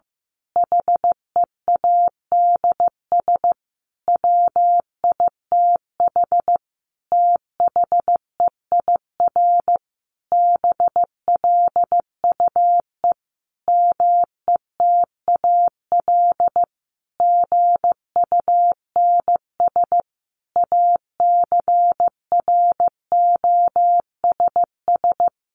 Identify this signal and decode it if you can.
Morse code, heard as a keyed tone, about 15 wpm, 700 Hz. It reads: HEADS WITH THEIR BLUE METAL GUNS ACROSS
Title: A Sound Of Thunder 15wpm